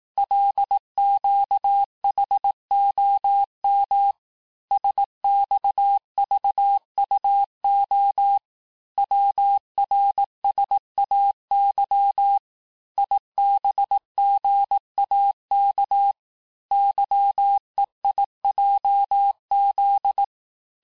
Morse code, heard as a keyed tone, about 18 wpm, 800 Hz. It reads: LQHOM SXVUO WRSAY IBGAK YEIJZ
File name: cours-du-lundi files / cours-1588-texte-chiffre-vitesse-17MPM